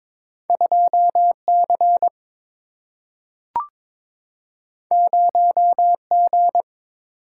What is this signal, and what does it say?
Morse code, heard as a keyed tone, about 22 wpm, 700 Hz. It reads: 2C 0G
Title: Single-Letter Single-Number 22wpm